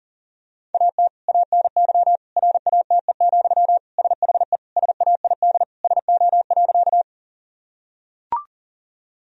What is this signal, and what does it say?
Morse code, heard as a keyed tone, about 37 wpm, 700 Hz. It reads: AT ANY RATE, SHE SAID SO.
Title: Sentences from Top 1000 Words - Review Part 4 - 40wpm